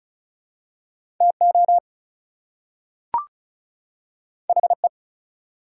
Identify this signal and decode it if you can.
Morse code, heard as a keyed tone, about 36 wpm, 700 Hz. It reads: TO HE
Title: Two Letter Words - 35wpm